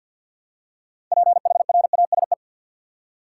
Morse code, heard as a keyed tone, about 50 wpm, 700 Hz. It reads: PHRASE
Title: Six Letter Words - 50wpm